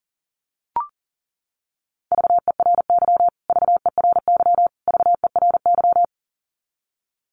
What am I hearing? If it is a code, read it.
Morse code, heard as a keyed tone, about 40 wpm, 700 Hz. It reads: VERY VERY VERY
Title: Top 500 Words Repeated 40wpm